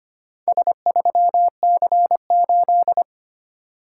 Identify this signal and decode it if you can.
Morse code, heard as a keyed tone, about 25 wpm, 700 Hz. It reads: S3C8
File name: practice / 25wpm / Four Letter-Number Combinations - 25wpm